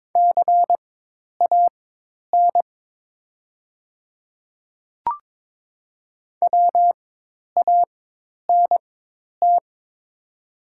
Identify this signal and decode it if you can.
Morse code, heard as a keyed tone, about 22 wpm, spaced as for 10 wpm, 690 Hz. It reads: CAN WANT
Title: 4x Character Spacing - Top 300 Words - 22wpm